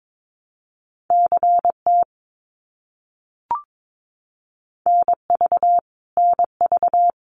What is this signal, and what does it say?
Morse code, heard as a keyed tone, about 22 wpm, 700 Hz. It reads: CT NV NV
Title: US States Repeated 3x 22wpm